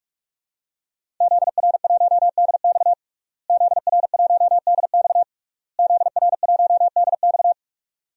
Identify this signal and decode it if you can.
Morse code, heard as a keyed tone, about 41 wpm, 700 Hz. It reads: ZR1DX ZR1DX ZR1DX
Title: DXCC-100 Call Signs - 5 Characters - Repeated 3x - 45wpm